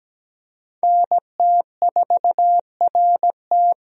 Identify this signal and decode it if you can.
Morse code, heard as a keyed tone, about 17 wpm, 700 Hz. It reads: NT4RT
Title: USA Call Signs - 5 Characters - Repeated 3x - 17wpm